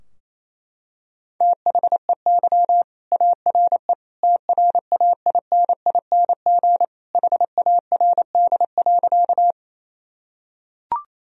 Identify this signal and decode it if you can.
Morse code, heard as a keyed tone, about 27 wpm, 700 Hz. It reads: THEY ARE TRAINING HARD.
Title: Sentences from Top 1000 Words - Review Part 4 - 28wpm